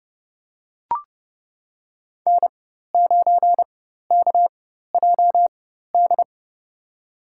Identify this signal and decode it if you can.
Morse code, heard as a keyed tone, about 30 wpm, spaced as for 14 wpm, 700 Hz. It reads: N9KJD
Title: USA Call Signs - 4x Character Spacing - 30wpm